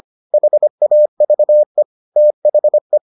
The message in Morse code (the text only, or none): HAVE THE